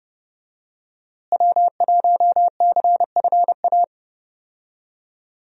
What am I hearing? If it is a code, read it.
Morse code, heard as a keyed tone, about 32 wpm, 700 Hz. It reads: W1CFA
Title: USA Call Signs - 5 Characters - Repeated 3x - 30wpm